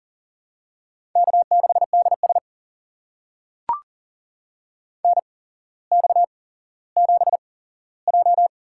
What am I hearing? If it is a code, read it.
Morse code, heard as a keyed tone, about 38 wpm, spaced as for 17 wpm, 700 Hz. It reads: K6DS NX7J
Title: USA Call Signs - 4 Characters - 8x Character Spacing - 40wpm